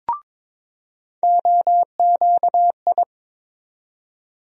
Morse code, heard as a keyed tone, about 22 wpm, 700 Hz. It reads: OQI
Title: USA Call Sign Postfixes - 3 Letters 22wpm